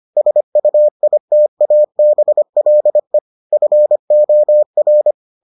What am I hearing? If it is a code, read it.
Morse code, heard as a keyed tone, about 25 wpm, 600 Hz. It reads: SUITABLE FOR